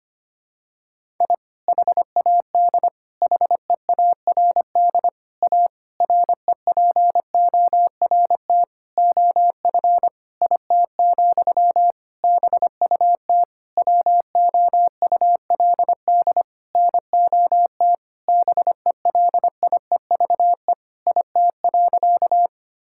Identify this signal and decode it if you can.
Morse code, heard as a keyed tone, about 25 wpm, 700 Hz. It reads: I HAD HEARD A REPORT OF IT, BUT WOULD NOT BELIEVE IT.
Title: Sentences from Top 1000 Words - Review Part 6 - 25wpm